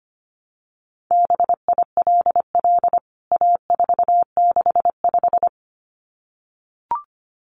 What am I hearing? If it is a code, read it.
Morse code, heard as a keyed tone, about 25 wpm, 700 Hz. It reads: BILL A465
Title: CWT Exchange 25wpm